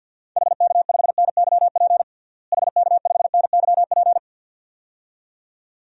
Morse code, heard as a keyed tone, about 53 wpm, 700 Hz. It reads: HK5NYP HK5NYP
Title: DXCC-100 Call Signs - 6 Characters - Repeated 2x - 50wpm